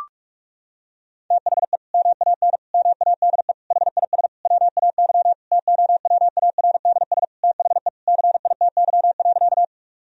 Morse code, heard as a keyed tone, about 47 wpm, 700 Hz. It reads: THE MAN MADE HIS WAY TOWARDS THE CITY.
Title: Sentences from Top 1000 Words - Review Part 2 - 45wpm